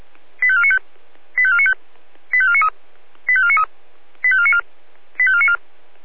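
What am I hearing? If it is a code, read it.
Five-tone selective call (ZVEI), heard as quick runs of five tones, tones of about 70 ms. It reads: 85495 85495 85493 85493 85494 85494
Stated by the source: ZVEI five-tone selective call received on SDR.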